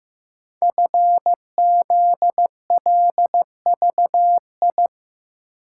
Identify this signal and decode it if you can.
Morse code, heard as a keyed tone, about 15 wpm, 700 Hz. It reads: FZLVI
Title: Code Groups - 5 Letters - 15wpm